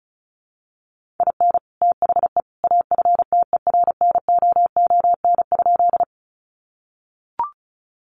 Morse code, heard as a keyed tone, about 34 wpm, 700 Hz. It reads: IN THE AFTERNOON?
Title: Sentences from Top 1500 Words 35wpm